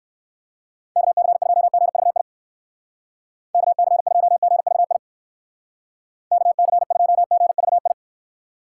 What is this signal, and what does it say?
Morse code, heard as a keyed tone, about 60 wpm, 700 Hz. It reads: KC2GVI KC2GVI KC2GVI